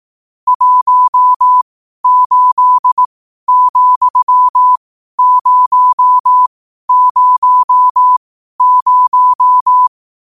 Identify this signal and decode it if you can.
Morse code, heard as a keyed tone, about 18 wpm, spaced as for 12 wpm, 1000 Hz. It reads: 18,000
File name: qotdfiles / 12wpm